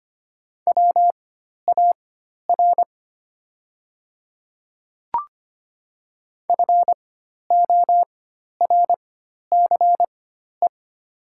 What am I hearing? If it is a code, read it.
Morse code, heard as a keyed tone, about 25 wpm, spaced as for 12 wpm, 700 Hz. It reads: WAR FORCE
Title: 4x Character Spacing - Top 500 Words - 25wpm